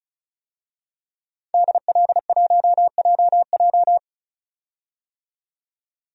Morse code, heard as a keyed tone, about 36 wpm, 700 Hz. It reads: DL1JJ
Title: DXCC-100 Call Signs - 5 Characters - 35wpm